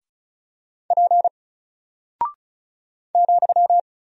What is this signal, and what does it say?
Morse code, heard as a keyed tone, about 36 wpm, 700 Hz. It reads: P ,
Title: Single Character (Letter:Number:Punctuation:Prosign) 35wpm